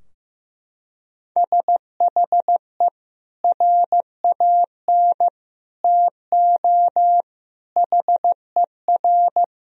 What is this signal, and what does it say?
Morse code, heard as a keyed tone, about 15 wpm, 700 Hz. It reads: SHE RAN TO HER
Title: Sentences from Top 1000 Words - Review Part 3 - 15wpm